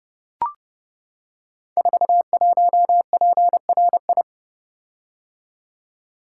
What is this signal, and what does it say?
Morse code, heard as a keyed tone, about 30 wpm, 700 Hz. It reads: 41PRI
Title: Code Groups - 5 Letter:Numbers - 30wpm